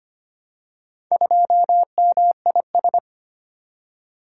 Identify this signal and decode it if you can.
Morse code, heard as a keyed tone, about 25 wpm, 700 Hz. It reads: 2MIS
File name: practice / 25wpm / Four Letter-Number Combinations - 25wpm